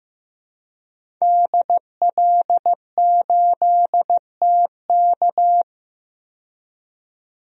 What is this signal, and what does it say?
Morse code, heard as a keyed tone, about 16 wpm, 700 Hz. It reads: DL8TK
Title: DXCC-100 Call Signs - 5 Characters - 15wpm